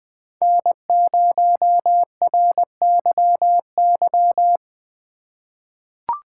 Morse code, heard as a keyed tone, about 20 wpm, 700 Hz. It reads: N0RYY
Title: USA Call Signs - 5 Characters - 3x Character Spacing - 20wpm